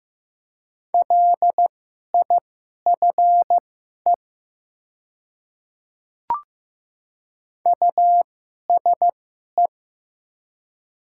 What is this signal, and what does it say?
Morse code, heard as a keyed tone, about 15 wpm, spaced as for 11 wpm, 700 Hz. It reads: LIFE USE
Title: 2x Character Spacing - Top 300 Words - 15wpm